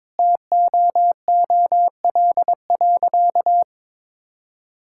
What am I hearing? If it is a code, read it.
Morse code, heard as a keyed tone, about 22 wpm, 700 Hz. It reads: TOOL.
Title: Sentences from Top 1000 Words - Review Part 6 - 22wpm